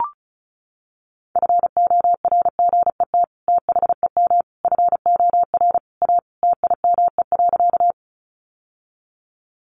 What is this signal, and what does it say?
Morse code, heard as a keyed tone, about 34 wpm, 700 Hz. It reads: FORGET THEM FOR A TIME.
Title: Sentences from Top 1200 Words 35wpm